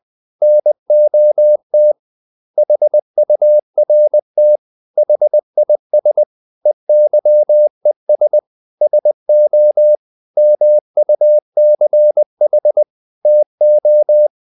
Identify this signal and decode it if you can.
Morse code, heard as a keyed tone, about 20 wpm, 600 Hz. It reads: NOT HURT HIS EYES SO MUCH TO
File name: the old man and the sea / the old man and the sea part 1 20wpm